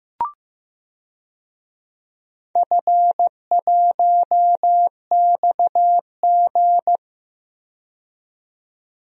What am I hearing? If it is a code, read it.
Morse code, heard as a keyed tone, about 15 wpm, 700 Hz. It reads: F1XG
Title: DXCC-100 Call Signs - 4 Characters - 15wpm